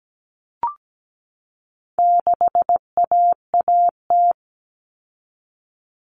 Morse code, heard as a keyed tone, about 17 wpm, 700 Hz. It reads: 6AAT
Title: Common Cut Numbers - 4-Digit 17wpm